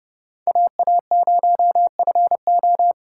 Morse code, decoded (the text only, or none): AA0FO